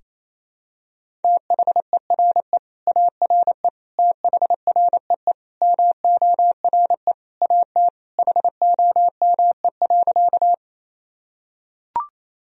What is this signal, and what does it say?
Morse code, heard as a keyed tone, about 28 wpm, 700 Hz. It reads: THERE ARE THREE MORE AT HOME.
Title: Sentences from Top 1000 Words - Review Part 1 - 28wpm